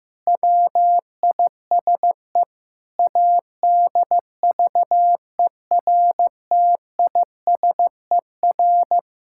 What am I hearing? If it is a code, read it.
Morse code, heard as a keyed tone, about 15 wpm, 700 Hz. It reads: WISE ADVERTISER
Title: Pangram Challenge v2 - 15wpm